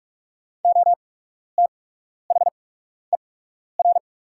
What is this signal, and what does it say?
Morse code, heard as a keyed tone, about 44 wpm, spaced as for 12 wpm, 700 Hz. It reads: OTHER
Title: 8x Character Spacing - Top 500 Words - 45wpm